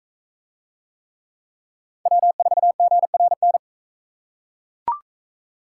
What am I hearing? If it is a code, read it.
Morse code, heard as a keyed tone, about 40 wpm, 700 Hz. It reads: W4GRN